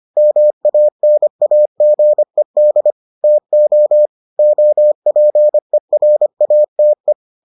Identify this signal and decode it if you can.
Morse code, heard as a keyed tone, about 25 wpm, 600 Hz. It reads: MANAGED TO OPERATE